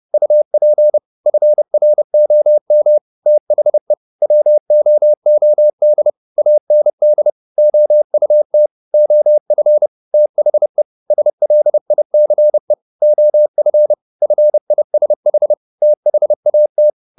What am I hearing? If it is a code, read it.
Morse code, heard as a keyed tone, about 30 wpm, 600 Hz. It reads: UP FROM THE WOOD AND OUT OF THE SLICE OF FISH THAT